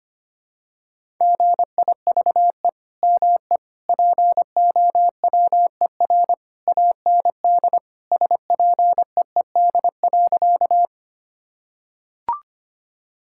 Morse code, heard as a keyed tone, about 25 wpm, 700 Hz. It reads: GIVE ME POWER AND SPEED.
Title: Sentences from Top 1000 Words - Review Part 5 - 25wpm